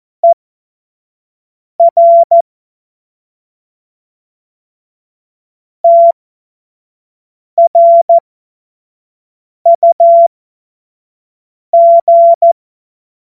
Morse code, heard as a keyed tone, about 14 wpm, spaced as for 5 wpm, 680 Hz. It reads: ER TRUG
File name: Morsekurs / Witze / Witz 31. August 2021, 25 F70